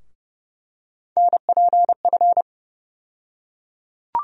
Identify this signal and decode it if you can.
Morse code, heard as a keyed tone, about 31 wpm, 700 Hz. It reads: NPF
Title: USA Call Sign Postfixes - 3 Letters 30wpm